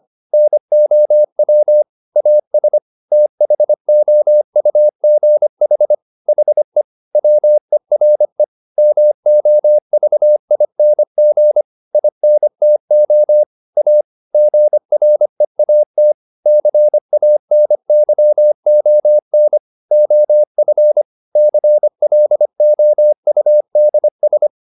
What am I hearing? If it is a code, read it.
Morse code, heard as a keyed tone, about 25 wpm, 600 Hz. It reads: NOW AS THOUGH HE WERE MOVING INTO A GREAT CANYON OF CLOUDS